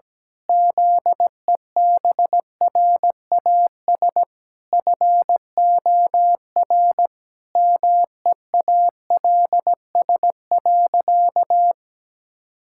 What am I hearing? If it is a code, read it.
Morse code, heard as a keyed tone, about 17 wpm, 700 Hz. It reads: ZEBRAS FOR MEALS.
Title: Pangram Challenge v2 - 17wpm